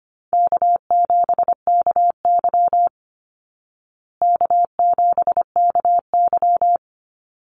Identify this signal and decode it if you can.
Morse code, heard as a keyed tone, about 25 wpm, 700 Hz. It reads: K7KY K7KY
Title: Easy Call Signs Repeated 2x - 25wpm